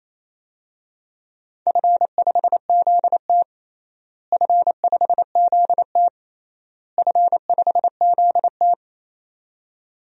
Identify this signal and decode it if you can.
Morse code, heard as a keyed tone, about 27 wpm, 700 Hz. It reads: F5ZT F5ZT F5ZT
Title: DXCC-100 Call Signs - 4 Characters - Repeated 3x - 28wpm